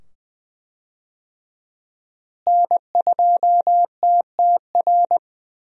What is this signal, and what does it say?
Morse code, heard as a keyed tone, about 20 wpm, 700 Hz. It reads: N2TTR
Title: USA Call Signs - 5 Characters - Repeated 2x - 20wpm